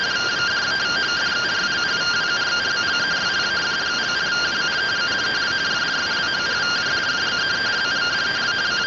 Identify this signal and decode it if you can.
Two-tone teleprinter signal, about 170 Hz shift.